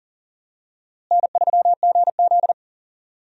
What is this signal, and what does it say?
Morse code, heard as a keyed tone, about 38 wpm, 700 Hz. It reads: N3GZ
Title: USA Call Signs - 4 Characters - Repeated 3x - 40wpm